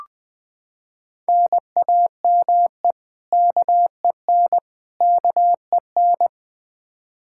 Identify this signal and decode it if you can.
Morse code, heard as a keyed tone, about 20 wpm, 700 Hz. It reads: NAME KEN KEN
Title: Practice QSO - Exchange 1 - Beginning - 20wpm